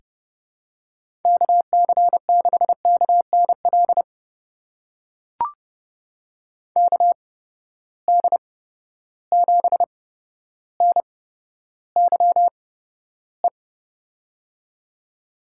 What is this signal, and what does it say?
Morse code, heard as a keyed tone, about 30 wpm, spaced as for 13 wpm, 700 Hz. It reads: KC6KNL KD7NYE
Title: USA Call Signs - 6 Characters - 8x Character Spacing - 30wpm